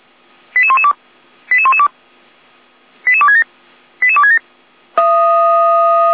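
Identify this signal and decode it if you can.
Five-tone selective call (ZVEI), heard as quick runs of five tones, tones of about 70 ms. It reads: 88112 88112 88267 88267
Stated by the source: ZVEI five-tone selective call received by SDR.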